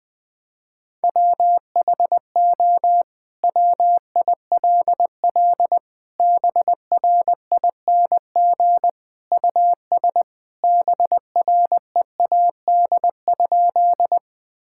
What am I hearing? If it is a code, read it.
Morse code, heard as a keyed tone, about 20 wpm, 700 Hz. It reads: WHO WILL BRING US BREAD?
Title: Sentences from Top 1600 Words 20wpm